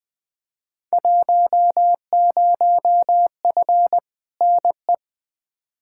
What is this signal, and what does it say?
Morse code, heard as a keyed tone, about 20 wpm, 700 Hz. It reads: 10F NE
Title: Field Day Exchanges Encore - 20wpm